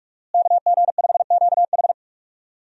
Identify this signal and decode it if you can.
Morse code, heard as a keyed tone, about 46 wpm, 700 Hz. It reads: KG5QH
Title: USA Call Signs - 5 Characters - Repeated 3x - 45wpm